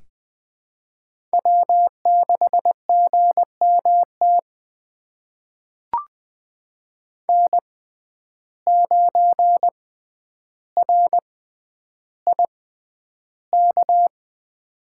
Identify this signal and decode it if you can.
Morse code, heard as a keyed tone, about 20 wpm, spaced as for 10 wpm, 700 Hz. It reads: W6GMT N9RIK
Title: USA Call Signs - 6x Character Spacing - 20wpm